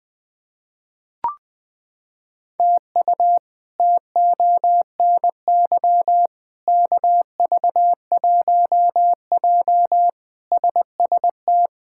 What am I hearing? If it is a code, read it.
Morse code, heard as a keyed tone, about 20 wpm, 700 Hz. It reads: TU TONY KV1J SST
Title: K1USN SST QSO Practice - 20wpm